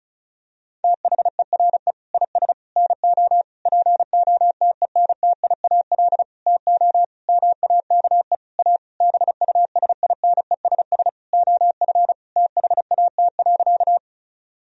Morse code, heard as a keyed tone, about 36 wpm, 700 Hz. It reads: THERE IS NO POTENTIAL TO MAKE A BUSINESS OF THAT.
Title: Sentences from Top 1000 Words - Review Part 4 - 35wpm